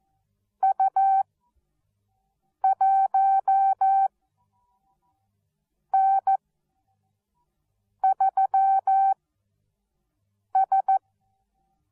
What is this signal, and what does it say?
Morse code, heard as a keyed tone, about 14 wpm, spaced as for 5 wpm, 770 Hz. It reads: U1N3S